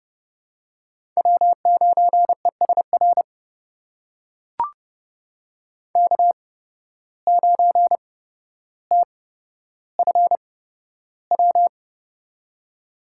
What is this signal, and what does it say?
Morse code, heard as a keyed tone, about 30 wpm, spaced as for 13 wpm, 700 Hz. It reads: W9ESR K9TFW
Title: USA Call Signs - 5 Characters - 8x Character Spacing - 30wpm